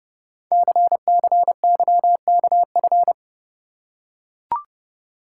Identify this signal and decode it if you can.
Morse code, heard as a keyed tone, about 30 wpm, 700 Hz. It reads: CCYKF
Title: Code Groups - 5 Letters - 30wpm